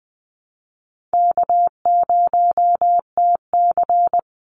Morse code, heard as a keyed tone, about 20 wpm, 700 Hz. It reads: K0TC
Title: Easy Call Signs Repeated 3x - 20wpm